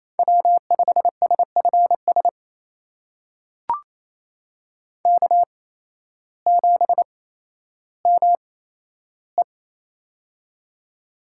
Morse code, heard as a keyed tone, about 28 wpm, spaced as for 13 wpm, 700 Hz. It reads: W5SFS K7ME
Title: USA Call Signs - 8x Character Spacing - 28wpm